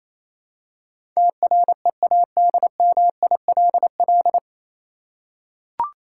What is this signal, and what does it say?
Morse code, heard as a keyed tone, about 28 wpm, 700 Hz. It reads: TREADMILL